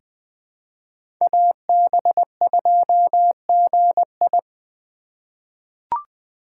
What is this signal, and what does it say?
Morse code, heard as a keyed tone, about 20 wpm, 700 Hz. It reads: AB2GI
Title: K1USN SST QSO Practice - 20wpm